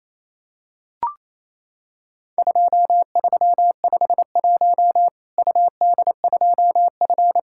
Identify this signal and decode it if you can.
Morse code, heard as a keyed tone, about 28 wpm, 700 Hz. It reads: 2351 UD2F
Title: Worked All Europe DX Contest - QTC Exchange - 28wpm